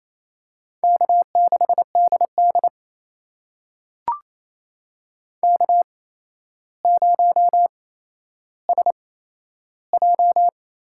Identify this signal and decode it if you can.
Morse code, heard as a keyed tone, about 28 wpm, spaced as for 12 wpm, 700 Hz. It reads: K6DD K0SJ
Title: USA Call Signs - 4 Characters - 8x Character Spacing - 28wpm